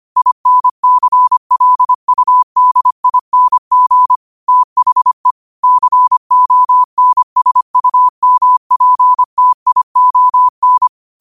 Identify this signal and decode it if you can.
Morse code, heard as a keyed tone, about 25 wpm, 1000 Hz. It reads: INCLUDING THE CONSUMPTION
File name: qotdfiles / 25wpm